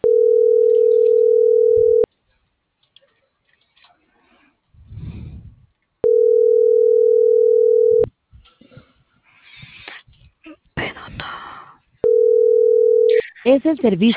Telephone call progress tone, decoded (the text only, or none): ringback tone